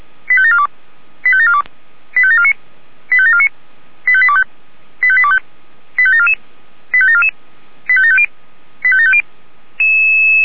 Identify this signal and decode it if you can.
Five-tone selective call (ZVEI), heard as quick runs of five tones, tones of about 70 ms. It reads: 85742 85742 85749 85749 85825 85825 85744 85744 85750 85750
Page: SDR/ZVEI